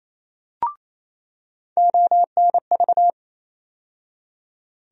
Morse code, heard as a keyed tone, about 28 wpm, 700 Hz. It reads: ONV
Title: USA Call Sign Postfixes - 3 Letters 28wpm